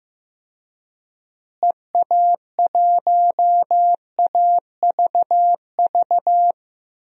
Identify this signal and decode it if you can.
Morse code, heard as a keyed tone, about 15 wpm, 700 Hz. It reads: EA1AVV
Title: DXCC-100 Call Signs - 6 Characters - Repeated 3x - 15wpm